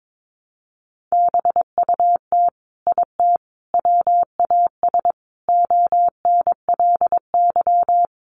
Morse code, heard as a keyed tone, about 22 wpm, 700 Hz. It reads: BUT IT WAS ONLY
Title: Sentences from Top 1000 Words 22wpm